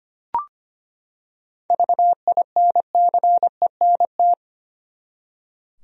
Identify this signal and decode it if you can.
Morse code, heard as a keyed tone, about 25 wpm, 700 Hz. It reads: VINCENT
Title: USA Common Names - 25wpm